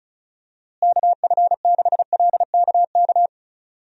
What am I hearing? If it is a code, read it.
Morse code, heard as a keyed tone, about 34 wpm, 700 Hz. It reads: KF6LKK
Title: USA Call Signs - 6 Characters - Repeated 2x - 35wpm